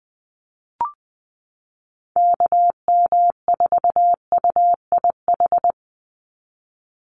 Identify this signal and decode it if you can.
Morse code, heard as a keyed tone, about 20 wpm, 700 Hz. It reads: KM4UIH
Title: USA Call Signs 20wpm